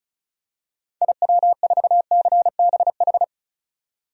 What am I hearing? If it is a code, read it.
Morse code, heard as a keyed tone, about 34 wpm, 700 Hz. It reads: IW4CBH
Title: DXCC-100 Call Signs - 6 Characters - 35wpm